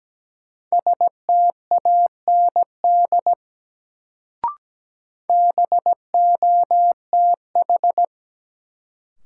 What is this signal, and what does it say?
Morse code, heard as a keyed tone, about 17 wpm, 700 Hz. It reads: STAND BOTH
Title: Sentences from Top 300 Words - New Word Drill - 17wpm